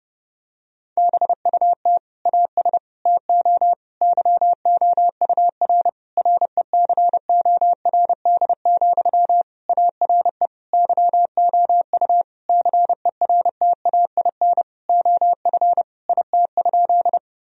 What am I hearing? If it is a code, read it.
Morse code, heard as a keyed tone, about 30 wpm, 700 Hz. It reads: BUT AS TO YOUR RECORD, ARE YOU CERTAIN OF IT?
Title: Sentences from Top 1000 Words - Review Part 3 - 30wpm